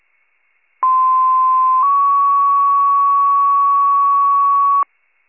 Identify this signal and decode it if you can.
Two-tone page, heard as a one-second tone, then a three-second tone, about 1040 then 1120 Hz. It is heard from 0.8 to 4.8 s.